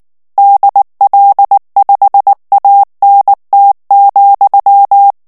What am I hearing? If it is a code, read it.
Morse code, heard as a keyed tone, about 19 wpm, 800 Hz. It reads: DL5ANT,
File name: morse mp3 / ALLEMAND texte clair 2 vitesse 18 mots minute